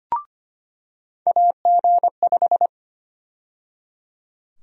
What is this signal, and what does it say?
Morse code, heard as a keyed tone, about 25 wpm, 700 Hz. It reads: AG5
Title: USA Call Sign Prefixes - 25wpm